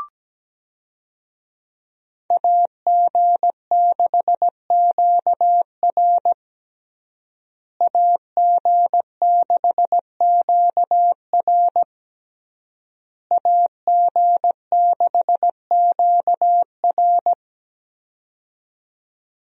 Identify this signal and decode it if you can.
Morse code, heard as a keyed tone, about 17 wpm, 700 Hz. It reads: AG6QR AG6QR AG6QR
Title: USA Call Signs - 5 Characters - Repeated 3x - 17wpm